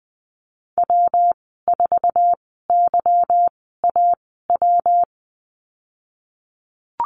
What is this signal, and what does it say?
Morse code, heard as a keyed tone, about 20 wpm, spaced as for 15 wpm, 700 Hz. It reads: W4YAW
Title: 179 Lesson 29g - Slash - Call Signs 20wpm